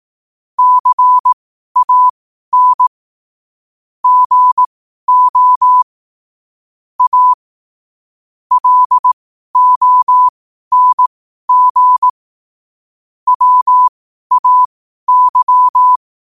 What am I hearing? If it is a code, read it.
Morse code, heard as a keyed tone, about 18 wpm, spaced as for 12 wpm, 1000 Hz. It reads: CAN GO A LONG WAY
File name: qotdfiles / 12wpm